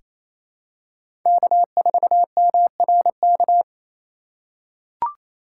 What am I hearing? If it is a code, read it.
Morse code, heard as a keyed tone, about 28 wpm, 700 Hz. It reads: K4MRK
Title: USA Call Signs - 5 Characters - 4x Character Spacing - 28wpm